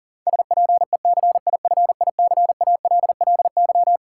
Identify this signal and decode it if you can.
Morse code, heard as a keyed tone, about 40 wpm, 700 Hz. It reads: SPECIFICALLY
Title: Sentences from Top 1100 Words - New Word Drill Alt - 40wpm